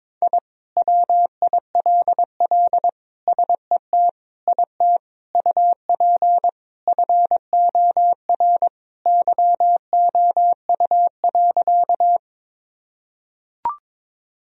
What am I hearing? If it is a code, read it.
Morse code, heard as a keyed tone, about 22 wpm, 700 Hz. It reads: I WILL SET IT UP FOR YOU.
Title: Sentences from Top 1000 Words - Review Part 1 - 22wpm